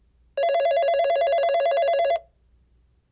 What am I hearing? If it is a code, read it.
There is no code.